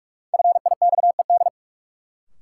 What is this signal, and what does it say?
Morse code, heard as a keyed tone, about 46 wpm, 700 Hz. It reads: FIXED
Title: Sentences from Top 1100 Words - New Word Drill Alt - 45wpm